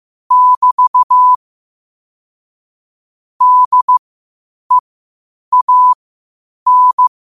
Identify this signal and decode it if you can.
Morse code, heard as a keyed tone, about 15 wpm, spaced as for 8 wpm, 1000 Hz. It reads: <BT> DEAN
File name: 8wpm